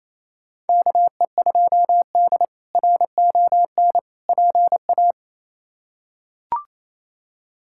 Morse code, heard as a keyed tone, about 28 wpm, 700 Hz. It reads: KE2D RON PA
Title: K1USN SST QSO Practice - 28wpm